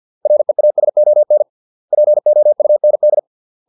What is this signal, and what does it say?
Morse code, heard as a keyed tone, about 50 wpm, 600 Hz. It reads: REASON POUND